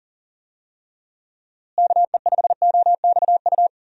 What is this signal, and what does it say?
Morse code, heard as a keyed tone, about 40 wpm, 700 Hz. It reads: KE5OXU
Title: USA Call Signs - 6 Characters - Repeated 3x - 40wpm